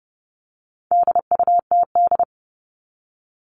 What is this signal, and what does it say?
Morse code, heard as a keyed tone, about 28 wpm, 700 Hz. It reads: DUTD
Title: All Cut Numbers - 4-Digit 30wpm